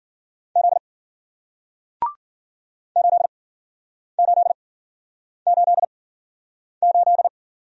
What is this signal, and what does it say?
Morse code, heard as a keyed tone, about 53 wpm, spaced as for 9 wpm, 700 Hz. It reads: Z 8888